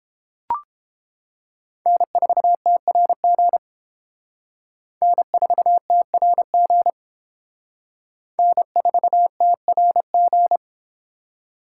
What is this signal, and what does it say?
Morse code, heard as a keyed tone, about 31 wpm, 700 Hz. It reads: N4TRG N4TRG N4TRG